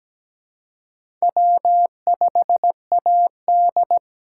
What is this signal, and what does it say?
Morse code, heard as a keyed tone, about 20 wpm, 700 Hz. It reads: W5AD
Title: Speed Racing - USA Call Signs 17wpm